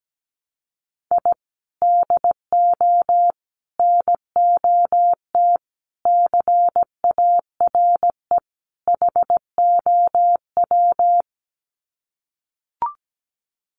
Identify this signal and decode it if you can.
Morse code, heard as a keyed tone, about 17 wpm, 700 Hz. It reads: I DO NOT CARE HOW
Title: Sets of 5 Words Encore2 17wpm